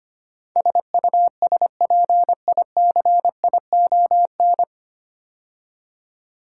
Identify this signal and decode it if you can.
Morse code, heard as a keyed tone, about 25 wpm, 700 Hz. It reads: SUSPICION
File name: practice / 25wpm / 9 Letter Words - First 3 Letters Given - 25wpm